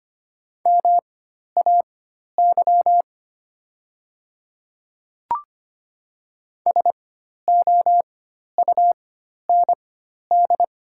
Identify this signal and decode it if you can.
Morse code, heard as a keyed tone, about 25 wpm, spaced as for 12 wpm, 700 Hz. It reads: MAY SOUND
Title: 4x Character Spacing - Top 100 Words - 25wpm